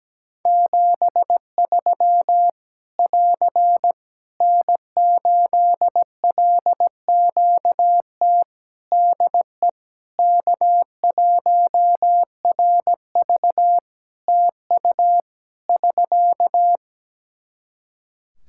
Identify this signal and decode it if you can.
Morse code, heard as a keyed tone, about 17 wpm, 700 Hz. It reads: 73 <AR> N8LQT DE K1RV TU <SK>
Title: K1USN SST QSO Practice 17wpm